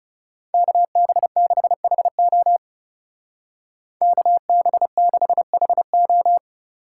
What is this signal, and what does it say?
Morse code, heard as a keyed tone, about 36 wpm, 700 Hz. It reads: KB6HO KB6HO
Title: Speed Racing - USA Call Signs 30wpm